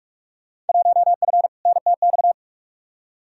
Morse code, heard as a keyed tone, about 46 wpm, 700 Hz. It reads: 1F NTX
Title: Field Day Exchanges Encore - 45wpm